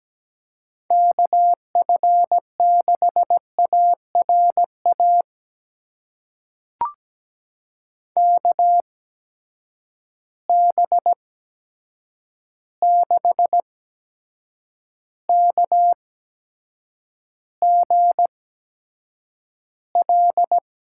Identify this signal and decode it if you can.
Morse code, heard as a keyed tone, about 17 wpm, spaced as for 7 wpm, 700 Hz. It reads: KF6ARA KB6KGL
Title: USA Call Signs - 6 Characters - 8x Character Spacing - 17wpm